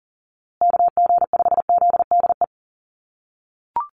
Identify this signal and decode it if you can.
Morse code, heard as a keyed tone, about 40 wpm, 700 Hz. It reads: KG5ZDE
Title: USA Call Signs Repeated 3x 40wpm